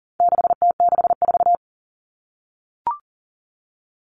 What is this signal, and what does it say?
Morse code, heard as a keyed tone, about 40 wpm, 700 Hz. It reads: 6T64
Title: All Cut Numbers - 4-Digit 40wpm